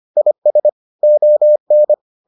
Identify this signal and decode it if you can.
Morse code, heard as a keyed tone, about 25 wpm, 600 Hz. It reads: IS ON